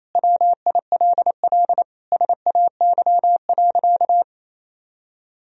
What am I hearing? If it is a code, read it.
Morse code, heard as a keyed tone, about 28 wpm, 700 Hz. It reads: WILL SAY.
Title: Sentences from Top 1000 Words - Review Part 1 - 28wpm